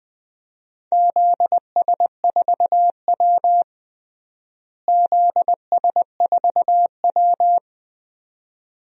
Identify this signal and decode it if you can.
Morse code, heard as a keyed tone, about 21 wpm, 700 Hz. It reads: ZS4W ZS4W
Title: DXCC-100 Call Signs - 4 Characters - Repeated 2x - 20wpm